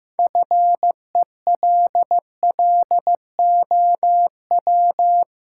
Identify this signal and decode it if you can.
Morse code, heard as a keyed tone, about 15 wpm, 700 Hz. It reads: FELLOW
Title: Sets of 5 Words Encore - New Word Drill - 15wpm